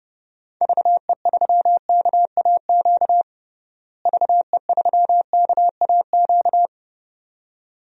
Morse code, heard as a keyed tone, about 32 wpm, 700 Hz. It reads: VE3KAQ VE3KAQ
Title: DXCC-100 Call Signs - 6 Characters - Repeated 2x - 30wpm